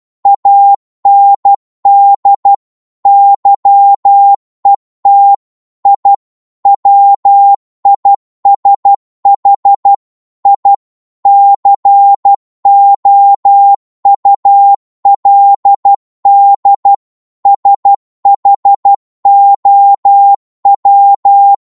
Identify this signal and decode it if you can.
Morse code, heard as a keyed tone, about 12 wpm, 800 Hz. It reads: AND YET I WISH I COULD SHOW
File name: Morse / psg-12wpm